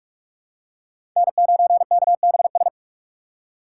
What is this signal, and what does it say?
Morse code, heard as a keyed tone, about 46 wpm, 700 Hz. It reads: N9KBS
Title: USA Call Signs - 5 Characters - 3x Character Spacing - 45wpm